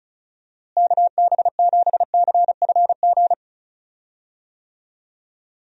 Morse code, heard as a keyed tone, about 34 wpm, 700 Hz. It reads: KB7CFG
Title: USA Call Signs - 6 Characters - Repeated 3x - 35wpm